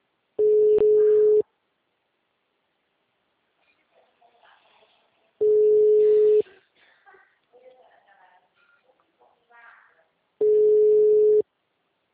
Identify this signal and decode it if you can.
Ringback tone (Europe, 425 Hz), with speech.